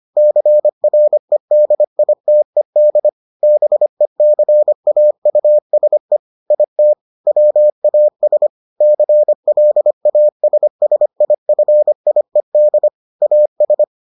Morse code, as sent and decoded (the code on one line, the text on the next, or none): -.-. .-. -.. .. - . -.. / -... -.-. .- ..- ... / .. - / .-- .- ... / -.-. .-.. .- ... ... .. ..-. .. -.. / .- ...
CREDITED BECAUSE IT WAS CLASSIFIED AS